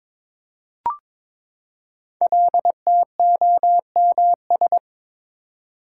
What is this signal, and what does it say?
Morse code, heard as a keyed tone, about 22 wpm, 690 Hz. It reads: LTOMS
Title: Code Groups - 5 Letter:Numbers - 22wpm